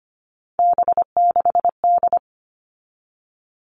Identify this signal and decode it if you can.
Morse code, heard as a keyed tone, about 25 wpm, 700 Hz. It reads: B6D
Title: All Cut Numbers - 3-Digit 25wpm